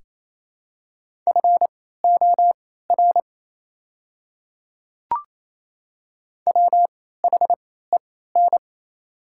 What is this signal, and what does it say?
Morse code, heard as a keyed tone, about 28 wpm, spaced as for 16 wpm, 700 Hz. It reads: FOR WHEN
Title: 3x Character Spacing - Top 100 Words - 28wpm